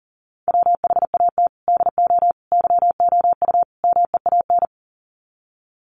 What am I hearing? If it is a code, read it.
Morse code, heard as a keyed tone, about 40 wpm, 700 Hz. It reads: WHAT DO YOU MEAN
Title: Sets of 4 Words 40wpm